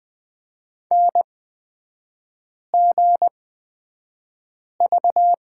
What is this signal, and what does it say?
Morse code, heard as a keyed tone, about 20 wpm, spaced as for 5 wpm, 700 Hz. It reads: NGV
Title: 236 Lesson 38e - Letter q - Three Characters 20wpm